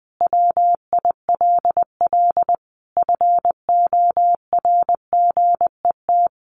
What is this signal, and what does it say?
Morse code, heard as a keyed tone, about 20 wpm, 700 Hz. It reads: WILL FORGET
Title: Sentences from Top 1200 Words 20wpm